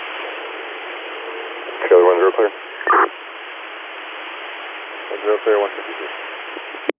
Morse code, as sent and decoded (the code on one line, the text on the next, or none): none